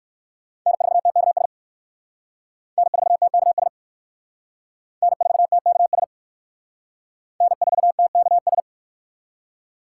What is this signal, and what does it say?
Morse code, heard as a keyed tone, about 56 wpm, 700 Hz. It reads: N4TKS N4TKS N4TKS N4TKS
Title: Speed Racing - USA Call Signs 45wpm